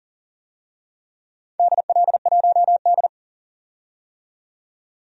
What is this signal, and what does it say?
Morse code, heard as a keyed tone, about 40 wpm, 700 Hz. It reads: DL1D
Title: DXCC-100 Call Signs - 4 Characters - 40wpm